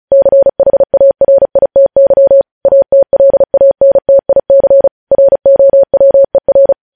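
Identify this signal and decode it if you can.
Morse code, heard as a keyed tone, about 36 wpm, 550 Hz. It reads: CHARITY ATLANTIC ROWER
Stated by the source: News headlines sent in Morse at 35 WPM.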